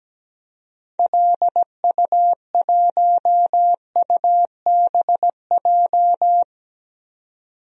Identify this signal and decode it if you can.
Morse code, heard as a keyed tone, about 17 wpm, 700 Hz. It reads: LU1UBJ
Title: DXCC-100 Call Signs - 6 Characters - 17wpm